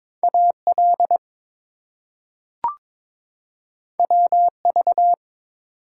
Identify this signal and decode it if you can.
Morse code, heard as a keyed tone, about 22 wpm, 700 Hz. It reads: AL WV
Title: US States and Canadian Provinces 22wpm